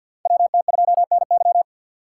Morse code, heard as a keyed tone, about 53 wpm, 690 Hz. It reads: WT2NY
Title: USA Call Signs - 5 Characters - 3x Character Spacing - 50wpm